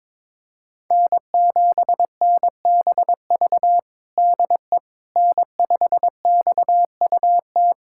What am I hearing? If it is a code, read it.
Morse code, heard as a keyed tone, about 22 wpm, 700 Hz. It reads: N7NBV DE N5XUT
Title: USA Callsign DE Callsign - 5 Characters - 22wpm